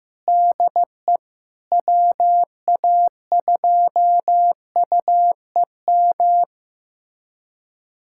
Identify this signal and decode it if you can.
Morse code, heard as a keyed tone, about 15 wpm, 700 Hz. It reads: DE WA2UEM
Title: USA Callsign DE Callsign - 6 Characters - 15wpm